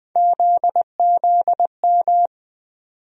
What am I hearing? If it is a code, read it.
Morse code, heard as a keyed tone, about 21 wpm, 700 Hz. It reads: ZZM
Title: USA Call Sign Postfixes - 3 Letters 20wpm